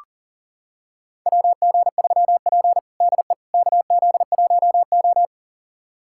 Morse code, heard as a keyed tone, about 40 wpm, 690 Hz. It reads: WG3P DE KZ1O
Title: USA Callsign DE Callsign - 4 Characters - 40wpm